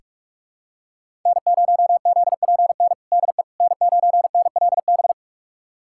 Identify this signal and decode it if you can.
Morse code, heard as a keyed tone, about 45 wpm, 700 Hz. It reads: N0ZPN DE N9NLB
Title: USA Callsign DE Callsign 45wpm